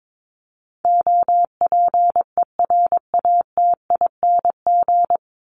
Morse code, heard as a keyed tone, about 22 wpm, 700 Hz. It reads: OPERATING
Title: ING Suffix - Top 100 Words with Base Words 22wpm